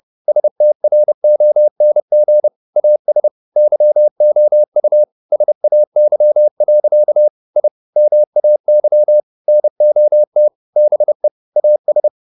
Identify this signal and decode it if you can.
Morse code, heard as a keyed tone, about 30 wpm, 600 Hz. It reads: STRONG AS YOU SAY. I MAY NOT BE AS
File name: the old man and the sea / the old man and the sea 1 30wpm